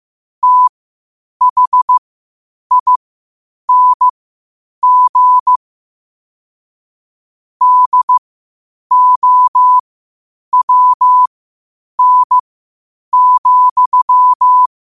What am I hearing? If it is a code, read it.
Morse code, heard as a keyed tone, about 15 wpm, spaced as for 8 wpm, 1000 Hz. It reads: THING DOWN,
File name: qotdfiles / 8wpm